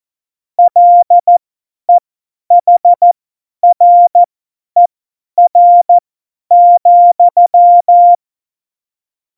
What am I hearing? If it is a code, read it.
Morse code, heard as a keyed tone, about 14 wpm, spaced as for 10 wpm, 700 Hz. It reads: LEHRER,
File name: Morsekurs / Witze / Witz 16. April 2019, 50 F70